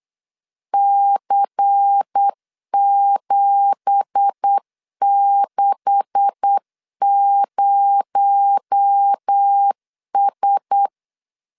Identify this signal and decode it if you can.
Morse code, heard as a keyed tone, about 8 wpm, 800 Hz. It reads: C760S
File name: LEZIONI / 9 2867 8wpm 3440